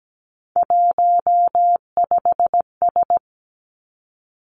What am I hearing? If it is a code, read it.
Morse code, heard as a keyed tone, about 17 wpm, 700 Hz. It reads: 15S
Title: Three Challenging Letter-Number Combinations 17wpm